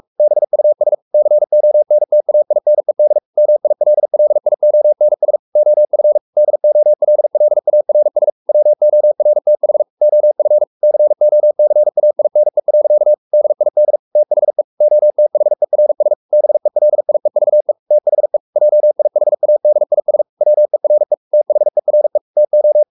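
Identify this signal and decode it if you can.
Morse code, heard as a keyed tone, about 44 wpm, 600 Hz. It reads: BUS CONTAINED MILLIONS OF DOLLARS WORTH OF COCAINE. DID THE OTHERS BELIEVE THE JIHADIS WERE THERE TO
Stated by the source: Morse code at 43 wpm sending Ken Follett's novel Never in English, from CW med Gnister.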